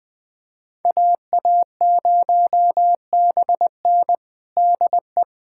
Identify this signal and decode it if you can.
Morse code, heard as a keyed tone, about 21 wpm, 700 Hz. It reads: AA0BN DE
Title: USA Callsign DE Callsign - 5 Characters - 20wpm